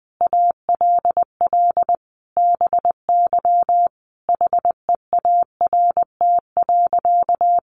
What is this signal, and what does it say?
Morse code, heard as a keyed tone, about 20 wpm, 700 Hz. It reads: ALL BY HEART.
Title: Sentences from Top 1200 Words 20wpm